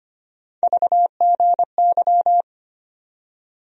Morse code, heard as a keyed tone, about 25 wpm, 700 Hz. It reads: VGY
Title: USA Call Sign Postfixes - 3 Letters 25wpm